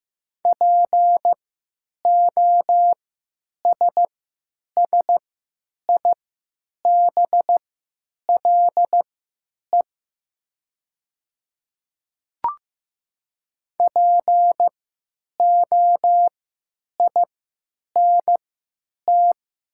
Morse code, heard as a keyed tone, about 15 wpm, spaced as for 9 wpm, 700 Hz. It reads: POSSIBLE POINT
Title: 3x Character Spacing - Top 500 Words - 15wpm